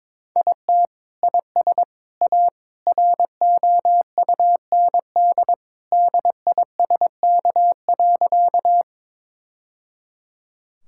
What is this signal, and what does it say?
Morse code, heard as a keyed tone, about 22 wpm, 700 Hz. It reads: IT IS A ROUND DISK.
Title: Sentences from Top 1000 Words - Review Part 6 - 22wpm